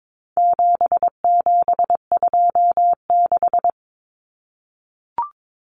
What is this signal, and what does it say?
Morse code, heard as a keyed tone, about 22 wpm, 700 Hz. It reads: 7726
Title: Numbers - 4-Digit 22wpm